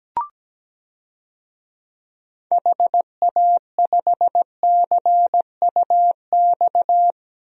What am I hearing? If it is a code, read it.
Morse code, heard as a keyed tone, about 17 wpm, 700 Hz. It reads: HA5CUX
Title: DXCC-100 Call Signs - 6 Characters - Repeated 2x - 17wpm